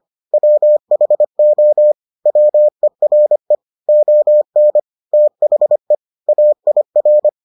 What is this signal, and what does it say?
Morse code, heard as a keyed tone, about 25 wpm, 600 Hz. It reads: WHO WERE ON THE AIR